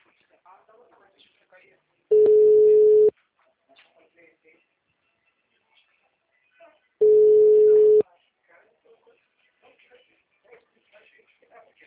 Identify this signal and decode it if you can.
Ringback tone (Europe, 425 Hz), with speech.